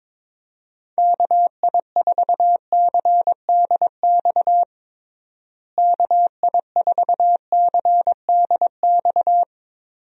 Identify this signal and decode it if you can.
Morse code, heard as a keyed tone, about 22 wpm, 700 Hz. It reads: KI4CDX KI4CDX
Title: USA Call Signs - 6 Characters - Repeated 2x - 22wpm